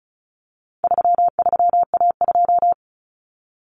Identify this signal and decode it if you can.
Morse code, heard as a keyed tone, about 34 wpm, 700 Hz. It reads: 33A2
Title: Common Cut Numbers - 4-Digit 35wpm